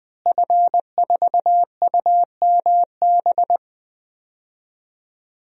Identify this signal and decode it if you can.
Morse code, heard as a keyed tone, about 20 wpm, 700 Hz. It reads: F4UMB
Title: DXCC-100 Call Signs - 5 Characters - Repeated 3x - 20wpm